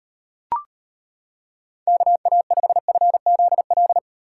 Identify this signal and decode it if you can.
Morse code, heard as a keyed tone, about 31 wpm, 700 Hz. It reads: KA5FZL